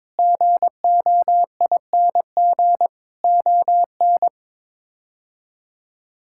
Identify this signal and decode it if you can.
Morse code, heard as a keyed tone, about 22 wpm, 700 Hz. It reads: GOING ON
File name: practice / 22wpm / Sets of 7 Words - 22wpm